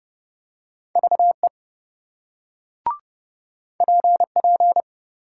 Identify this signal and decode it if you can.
Morse code, heard as a keyed tone, about 28 wpm, 700 Hz. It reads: VE PP